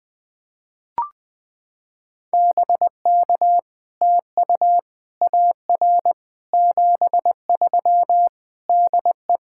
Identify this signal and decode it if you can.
Morse code, heard as a keyed tone, about 20 wpm, 700 Hz. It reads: BK TU AR 73 DE
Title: POTA QSO Practice - 20wpm